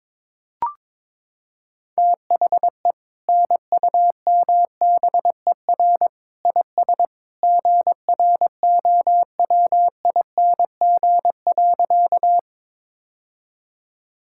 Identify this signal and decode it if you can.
Morse code, heard as a keyed tone, about 22 wpm, 700 Hz. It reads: THE NUMBER IS GROWING.
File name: practice / 22wpm / Sentences with 2-Word Phrases - 22wpm